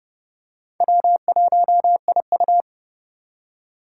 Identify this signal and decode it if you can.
Morse code, heard as a keyed tone, about 29 wpm, 700 Hz. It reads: W1IU